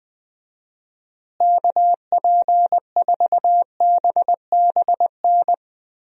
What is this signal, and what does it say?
Morse code, heard as a keyed tone, about 20 wpm, 700 Hz. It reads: KP4BBN
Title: DXCC-100 Call Signs - 6 Characters - Repeated 2x - 20wpm